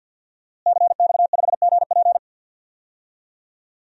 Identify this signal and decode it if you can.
Morse code, heard as a keyed tone, about 51 wpm, 700 Hz. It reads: CX5GP
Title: DXCC-100 Call Signs - 5 Characters - Repeated 2x - 50wpm